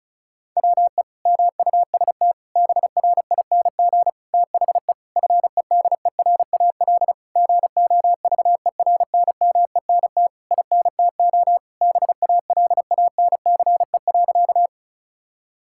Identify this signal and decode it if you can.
Morse code, heard as a keyed tone, about 34 wpm, 700 Hz. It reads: WE MUST BRING THE FEDERAL GOVERNMENT INTO BALANCE.
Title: Sentences from Top 1000 Words - Review Part 6 - 35wpm